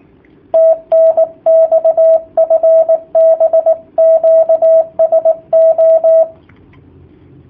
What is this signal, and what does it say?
Morse code, heard as a keyed tone, about 18 wpm, 650 Hz, with radio noise. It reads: TNX FB QSO
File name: jh8cbh / 2019cw / sp18- w